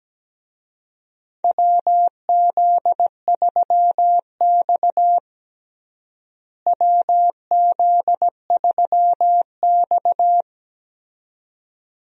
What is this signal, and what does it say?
Morse code, heard as a keyed tone, about 17 wpm, 700 Hz. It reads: WZ3X WZ3X
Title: USA Call Signs - 4 Characters - Repeated 2x - 17wpm